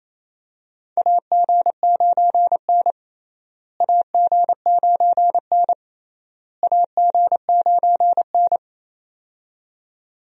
Morse code, heard as a keyed tone, about 28 wpm, 700 Hz. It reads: AG9N AG9N AG9N
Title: USA Call Signs - 4 Characters - Repeated 3x - 28wpm